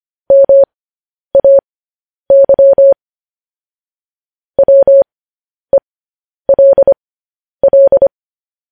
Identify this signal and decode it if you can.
Morse code, heard as a keyed tone, about 25 wpm, spaced as for 10 wpm, 550 Hz. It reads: MAY WELL
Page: news headlines in morse code at 10 wpm